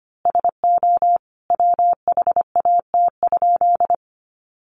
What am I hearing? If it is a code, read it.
Morse code, heard as a keyed tone, about 25 wpm, 700 Hz. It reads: SO WHAT?
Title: A Sound Of Thunder 25wpm